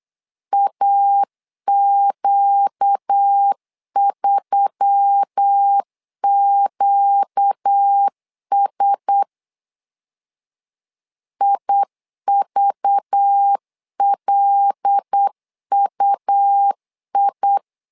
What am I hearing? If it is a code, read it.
Morse code, heard as a keyed tone, about 8 wpm, 800 Hz. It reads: AQ3QS IVLUI
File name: LEZIONI / 5 LBVC / 5 LBV 8wpm 2837